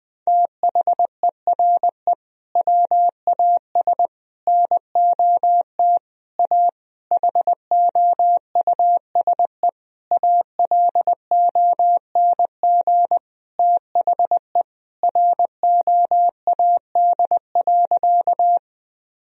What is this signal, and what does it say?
Morse code, heard as a keyed tone, about 20 wpm, 700 Hz. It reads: THERE WAS NOT A HOUSE ALONG THE ROAD.
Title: Sentences from Top 1000 Words - Review Part 2 - 20wpm